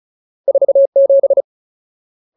Morse code, heard as a keyed tone, about 34 wpm, 550 Hz. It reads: 47
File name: Ultimate Word List / 35WPM Ultimate